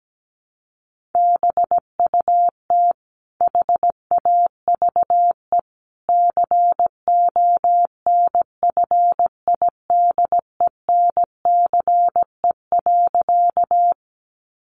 Morse code, heard as a keyed tone, about 17 wpm, 700 Hz. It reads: BUT HAVE CONFIDENCE.
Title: Sentences from Top 1500 Words 17wpm